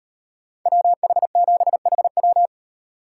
Morse code, heard as a keyed tone, about 38 wpm, 700 Hz. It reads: WH7HW